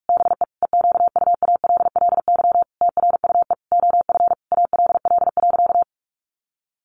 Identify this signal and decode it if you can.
Morse code, heard as a keyed tone, about 44 wpm, 700 Hz. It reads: BE EQUALLY TRUE OF ALL.
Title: Sentences from Top 1400 Words 45wpm